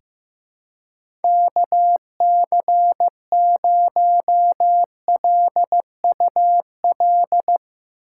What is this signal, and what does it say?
Morse code, heard as a keyed tone, about 15 wpm, 700 Hz. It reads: KC0LUL
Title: USA Call Signs - 6 Characters - 3x Character Spacing - 15wpm